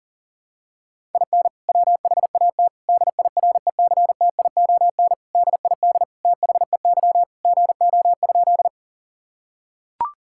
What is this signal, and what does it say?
Morse code, heard as a keyed tone, about 40 wpm, 700 Hz. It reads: IN WHAT DIRECTION DID THEY GO?
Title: Sentences from Top 1000 Words - Review Part 5 - 40wpm